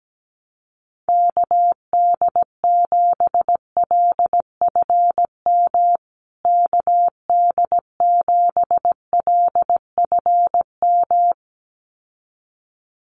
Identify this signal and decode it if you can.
Morse code, heard as a keyed tone, about 17 wpm, 700 Hz. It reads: KD7LFM KD7LFM
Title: USA Call Signs Repeated 2x 17wpm